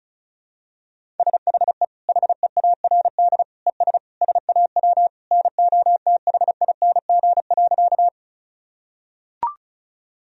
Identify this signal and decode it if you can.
Morse code, heard as a keyed tone, about 34 wpm, 700 Hz. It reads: SHE HEARD ES SAW NOTHING.
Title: Sentences from Top 500 Words and Abbreviations 35wpm